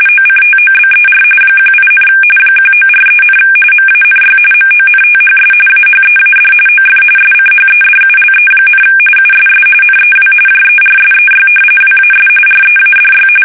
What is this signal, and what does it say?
RTTY, about 45.45 baud. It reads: OCTOBER 18, 1943, MANY MONTHS AHEAD OF SCHEDULE, USS NEW JERSEY WAS READY
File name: fsk850 / battleshipnj